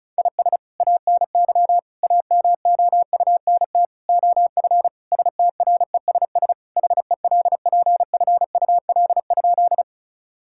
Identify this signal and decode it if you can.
Morse code, heard as a keyed tone, about 36 wpm, 700 Hz. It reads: IS ANY AMOUNT OF STRESS HELPFUL?
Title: Sentences from Top 1000 Words - Review Part 6 - 35wpm